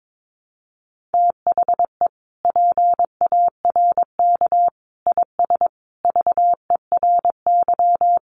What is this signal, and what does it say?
Morse code, heard as a keyed tone, about 22 wpm, 700 Hz. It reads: THE PARK IS VERY
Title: Sentences from Top 1300 Words 22wpm